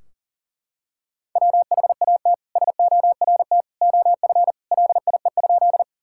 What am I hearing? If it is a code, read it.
Morse code, heard as a keyed tone, about 37 wpm, 700 Hz. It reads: WHAT SORT OF LIE?
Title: Sentences from Top 1600 Words 40wpm